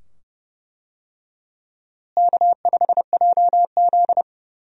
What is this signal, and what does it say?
Morse code, heard as a keyed tone, about 31 wpm, 700 Hz. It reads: K5JZ
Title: USA Call Signs - 4 Characters - Repeated 2x - 30wpm